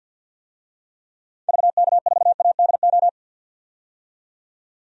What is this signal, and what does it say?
Morse code, heard as a keyed tone, about 52 wpm, 700 Hz. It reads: VK4ADO
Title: DXCC-100 Call Signs - 6 Characters - 50wpm